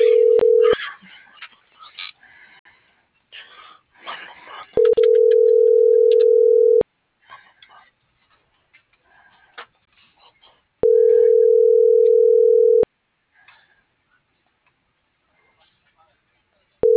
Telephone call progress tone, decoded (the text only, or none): ringback tone